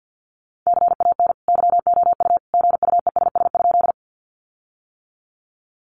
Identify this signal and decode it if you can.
Morse code, heard as a keyed tone, about 50 wpm, 700 Hz. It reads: CAN YOU GUESS?
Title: Sentences from Top 1500 Words 50wpm